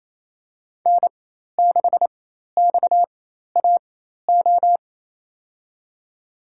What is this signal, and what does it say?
Morse code, heard as a keyed tone, about 28 wpm, spaced as for 13 wpm, 700 Hz. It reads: N6XAO
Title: USA Call Signs - 5 Characters - 4x Character Spacing - 28wpm